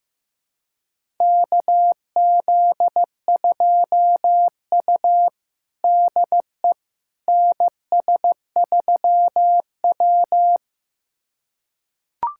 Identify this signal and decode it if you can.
Morse code, heard as a keyed tone, about 15 wpm, 700 Hz. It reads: KZ2U DE NS3W
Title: USA Callsign DE Callsign - 4 Characters - 15wpm